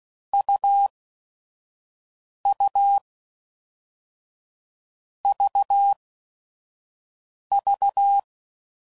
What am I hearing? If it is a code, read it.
Morse code, heard as a keyed tone, about 16 wpm, spaced as for 5 wpm, 800 Hz. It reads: UU VV
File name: Mixed / Short Runs / AUV AUV4